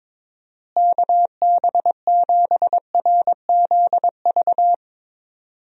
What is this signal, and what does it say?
Morse code, heard as a keyed tone, about 22 wpm, 700 Hz. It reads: KB7RZV
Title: USA Call Signs - 6 Characters - Repeated 2x - 22wpm